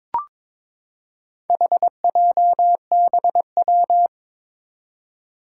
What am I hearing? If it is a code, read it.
Morse code, heard as a keyed tone, about 22 wpm, 700 Hz. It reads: HJBW
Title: Four Challenging Letter-Number Combinations 22wpm